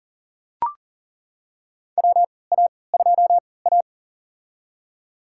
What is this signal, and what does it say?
Morse code, heard as a keyed tone, about 40 wpm, spaced as for 23 wpm, 700 Hz. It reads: WA2A
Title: USA Call Signs - 4 Characters - 3x Character Spacing - 40wpm